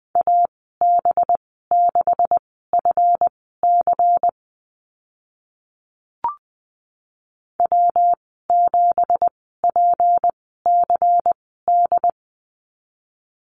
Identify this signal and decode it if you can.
Morse code, heard as a keyed tone, about 20 wpm, spaced as for 14 wpm, 700 Hz. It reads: AB6FC W7PCD
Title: 213 Lesson 34g - Number 8 - Call Signs 20wpm